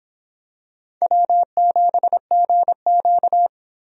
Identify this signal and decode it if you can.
Morse code, heard as a keyed tone, about 25 wpm, 700 Hz. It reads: W7GQ